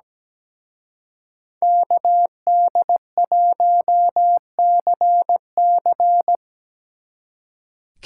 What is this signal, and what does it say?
Morse code, heard as a keyed tone, about 20 wpm, 700 Hz. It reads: KD1CC
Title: Speed Racing - USA Call Signs 17wpm